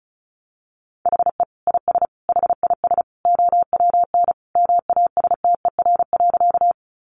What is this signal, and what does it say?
Morse code, heard as a keyed tone, about 34 wpm, 700 Hz. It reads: HE IS HIS OWN MASTER.
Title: Sentences from Top 1300 Words 35wpm